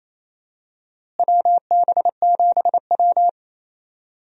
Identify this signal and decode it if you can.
Morse code, heard as a keyed tone, about 28 wpm, 700 Hz. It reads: WB7W
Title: USA Call Signs - 4 Characters - Repeated 3x - 28wpm